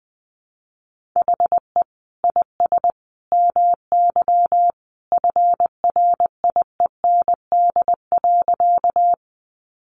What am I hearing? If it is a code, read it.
Morse code, heard as a keyed tone, about 20 wpm, 700 Hz. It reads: HE IS MY FRIEND.
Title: Sentences from Top 300 Words 20wpm